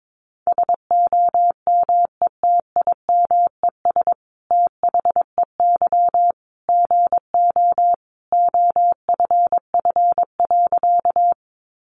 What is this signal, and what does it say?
Morse code, heard as a keyed tone, about 22 wpm, 700 Hz. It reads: SOMETIMES THEY GO OFF.
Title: Sentences from Top 600 Words 22wpm